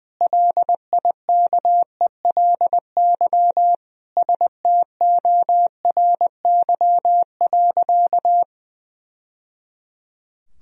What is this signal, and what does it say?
Morse code, heard as a keyed tone, about 20 wpm, 700 Hz. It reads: LIKELY STORY.
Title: Sentences from Top 1000 Words - Review Part 4 - 20wpm